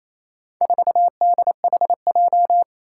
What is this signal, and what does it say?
Morse code, heard as a keyed tone, about 28 wpm, 700 Hz. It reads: 4DHJ
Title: Four Challenging Letter-Number Combinations 28wpm